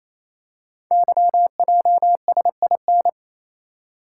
Morse code, heard as a keyed tone, about 27 wpm, 700 Hz. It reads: YJSIN
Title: Code Groups - 5 Letters - 28wpm